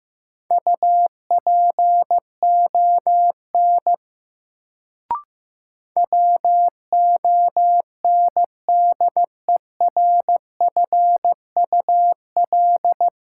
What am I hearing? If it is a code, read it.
Morse code, heard as a keyed tone, about 15 wpm, 700 Hz. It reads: UPON WONDERFUL
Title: Sentences from Top 1100 Words - New Word Drill Alt - 15wpm